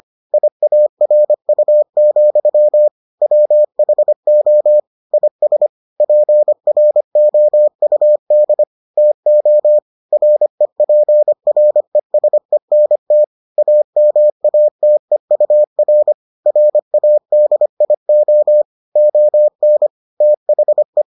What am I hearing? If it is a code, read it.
Morse code, heard as a keyed tone, about 25 wpm, 600 Hz. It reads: IARU, WHO IS PROUD TO REPRESENT AMATEUR RADIO ON THE